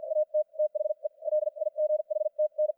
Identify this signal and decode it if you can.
Morse code, heard as a keyed tone, about 50 wpm, spaced as for 41 wpm, 610 Hz, with radio noise. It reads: AT THE LIGHTN
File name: archived / CRSnet